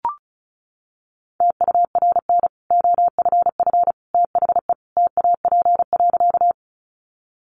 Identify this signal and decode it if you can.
Morse code, heard as a keyed tone, about 34 wpm, 700 Hz. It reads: TURN OFF THE TAP.
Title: Sentences from Top 1400 Words 35wpm